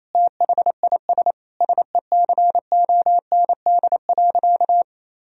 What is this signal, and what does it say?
Morse code, heard as a keyed tone, about 28 wpm, 700 Hz. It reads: THIS SECOND.
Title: Sentences from Top 1000 Words - Review Part 1 - 28wpm